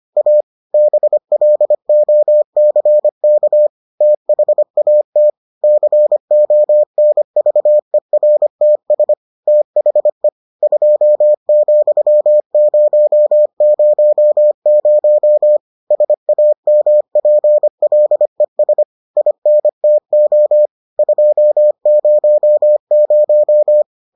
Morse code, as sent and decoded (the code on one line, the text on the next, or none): .- / -... .-.. --- -.-. -.- / - .... .- - / -.-. --- -. ...- . .-. - ... / - .... / ..--- --..-- ----- ----- ----- / ... .- -- .--. .-.. ... / .. -. - --- / ..--- ----- -----
A BLOCK THAT CONVERTS THE 2,000 SAMPLES INTO 200